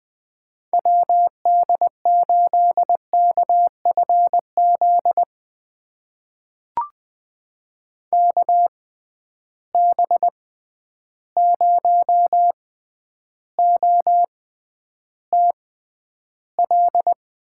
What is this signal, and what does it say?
Morse code, heard as a keyed tone, about 20 wpm, spaced as for 10 wpm, 700 Hz. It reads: WD8KFZ KB0OTL
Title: USA Call Signs - 6 Characters - 6x Character Spacing - 20wpm